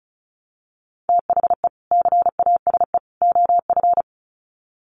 Morse code, heard as a keyed tone, about 34 wpm, 700 Hz. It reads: THE CASE OF
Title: Sets of 3 Words 35wpm